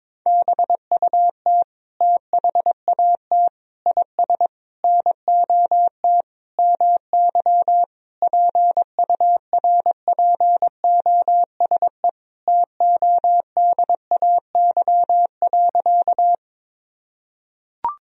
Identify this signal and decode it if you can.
Morse code, heard as a keyed tone, about 22 wpm, 700 Hz. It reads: BUT THAT IS NOT MY PURPOSE TODAY.
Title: Sentences from Top 1000 Words - Review Part 4 - 22wpm